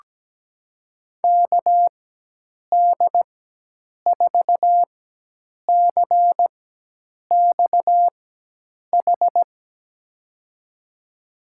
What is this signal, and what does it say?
Morse code, heard as a keyed tone, about 17 wpm, spaced as for 8 wpm, 700 Hz. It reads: KD4CXH
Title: USA Call Signs - 6 Characters - 4x Character Spacing - 17wpm